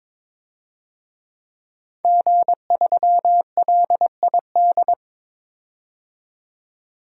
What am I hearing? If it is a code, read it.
Morse code, heard as a keyed tone, about 22 wpm, 700 Hz. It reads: G3LID
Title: DXCC-100 Call Signs - 5 Characters - 22wpm